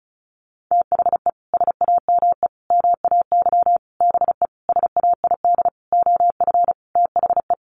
Morse code, heard as a keyed tone, about 34 wpm, 700 Hz. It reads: THE SAME MAY BE SAID OF THE
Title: Sentences from Top 1100 Words 35wpm